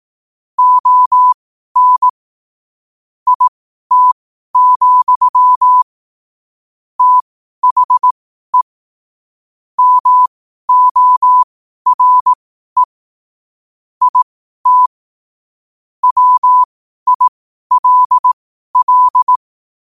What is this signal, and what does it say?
Morse code, heard as a keyed tone, about 18 wpm, spaced as for 12 wpm, 1000 Hz. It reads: ON IT, THE MORE IT WILL
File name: qotdfiles / 12wpm